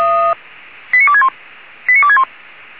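Five-tone selective call (ZVEI), heard as quick runs of five tones, tones of about 70 ms. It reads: 89261 89261
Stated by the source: ZVEI five-tone selective call received by SDR.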